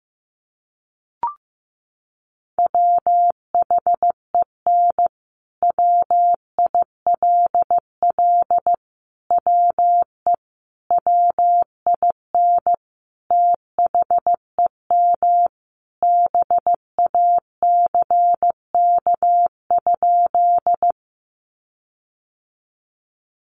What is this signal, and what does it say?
Morse code, heard as a keyed tone, about 15 wpm, 700 Hz. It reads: WHEN WILL WE WIN THEM BACK?
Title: Sentences from Top 1500 Words 15wpm